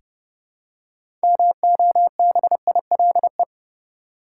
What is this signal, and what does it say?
Morse code, heard as a keyed tone, about 30 wpm, 700 Hz. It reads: MOBILE
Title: Six Letter Words - 30wpm